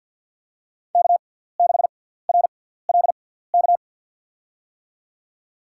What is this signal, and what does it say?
Morse code, heard as a keyed tone, about 53 wpm, spaced as for 17 wpm, 700 Hz. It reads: K6RLK
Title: USA Call Signs - 5 Characters - 6x Character Spacing - 50wpm